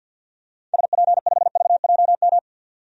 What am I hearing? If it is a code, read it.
Morse code, heard as a keyed tone, about 50 wpm, 700 Hz. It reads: SP5VJM
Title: DXCC-100 Call Signs - 6 Characters - 50wpm